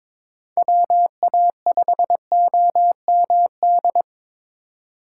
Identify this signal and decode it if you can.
Morse code, heard as a keyed tone, about 22 wpm, 700 Hz. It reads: WA5OMD
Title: USA Call Signs - 3x Character Spacing - 22wpm